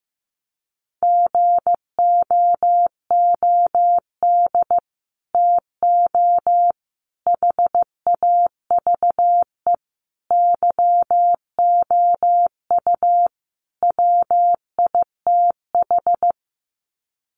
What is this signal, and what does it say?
Morse code, heard as a keyed tone, about 14 wpm, 700 Hz. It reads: GOOD TO HAVE YOU WITH
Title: Sets of 5 Words Encore2 15wpm